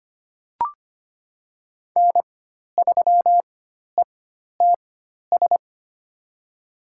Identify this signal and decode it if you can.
Morse code, heard as a keyed tone, about 25 wpm, spaced as for 12 wpm, 700 Hz. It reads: N3ETS
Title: USA Call Signs - 5 Characters - 4x Character Spacing - 25wpm